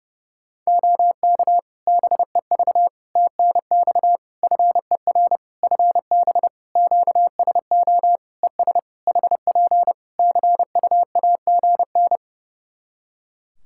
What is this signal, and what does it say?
Morse code, heard as a keyed tone, about 30 wpm, 700 Hz. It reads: OK BEV TNX FER FB QSO ES HP CUAGN
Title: Practice QSO - Exchange 4 - Ending - 30wpm